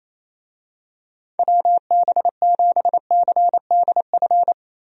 Morse code, heard as a keyed tone, about 28 wpm, 700 Hz. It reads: WB7CDF
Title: USA Call Signs - 6 Characters - Repeated 3x - 28wpm